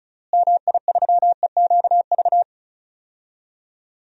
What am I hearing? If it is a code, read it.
Morse code, heard as a keyed tone, about 34 wpm, 700 Hz. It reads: MI3EQV
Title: DXCC-100 Call Signs - 6 Characters - Repeated 3x - 35wpm